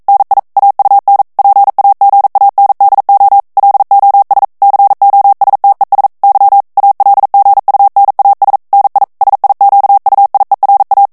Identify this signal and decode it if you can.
Morse code, heard as a keyed tone, about 42 wpm, 800 Hz. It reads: NI AUN PAGANDO LOS COSTES Y ALGUNAS NI SIQUIERA
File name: morse mp3 / ESPAGNOL TEXTE EN CLAIR 1 VITESSE 40 MOTS MINUTES